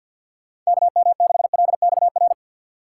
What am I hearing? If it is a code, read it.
Morse code, heard as a keyed tone, about 62 wpm, 700 Hz. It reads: KM6LXR
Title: Speed Racing - USA 2x3 Call Signs 50wpm